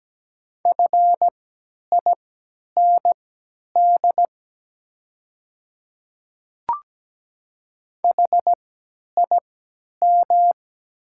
Morse code, heard as a keyed tone, about 17 wpm, spaced as for 10 wpm, 690 Hz. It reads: FIND HIM
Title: 3x Character Spacing - Top 500 Words - 17wpm